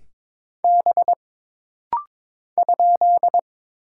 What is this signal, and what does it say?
Morse code, heard as a keyed tone, about 22 wpm, 700 Hz. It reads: B ?
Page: This rapid-fire timing is geared to push your ICR (Instant Character Recognition) skills.